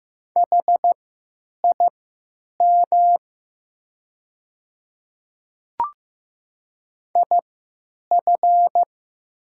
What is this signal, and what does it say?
Morse code, heard as a keyed tone, about 15 wpm, spaced as for 9 wpm, 700 Hz. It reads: HIM IF
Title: 3x Character Spacing - Top 100 Words - 15wpm